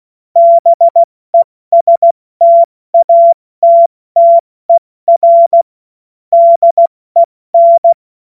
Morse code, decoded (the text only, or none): BESTATTER DEN